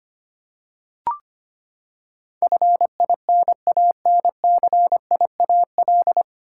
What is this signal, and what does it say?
Morse code, heard as a keyed tone, about 25 wpm, 700 Hz. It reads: FINANCIAL
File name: Nine Letter Words - 25wpm